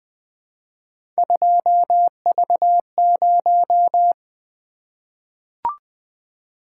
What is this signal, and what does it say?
Morse code, heard as a keyed tone, about 20 wpm, 700 Hz. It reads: 2V0
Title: Three Letter-Number Combinations - 20wpm